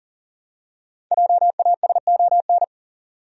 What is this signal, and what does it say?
Morse code, heard as a keyed tone, about 40 wpm, 700 Hz. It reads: JASON
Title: USA Common Names - 40wpm